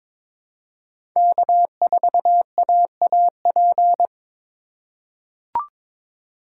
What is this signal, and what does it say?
Morse code, heard as a keyed tone, about 22 wpm, 700 Hz. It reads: K4AAP
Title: USA Call Signs - 5 Characters - 3x Character Spacing - 22wpm